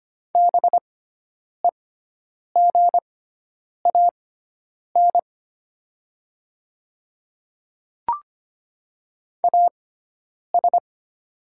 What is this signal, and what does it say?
Morse code, heard as a keyed tone, about 25 wpm, spaced as for 9 wpm, 690 Hz. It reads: BEGAN AS
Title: 6x Character Spacing - Top 300 Words - 25wpm